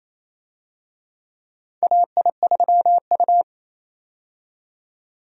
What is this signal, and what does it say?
Morse code, heard as a keyed tone, about 28 wpm, 700 Hz. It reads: AI3U
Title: USA Call Signs - 4 Characters - 28wpm